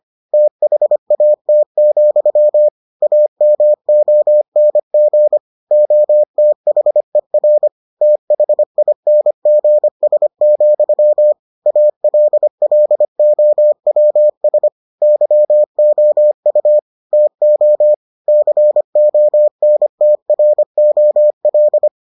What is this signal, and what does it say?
Morse code, heard as a keyed tone, about 25 wpm, 600 Hz. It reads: THAT, AMONG OTHER THINGS, ALLOWS YOU TO CONTROL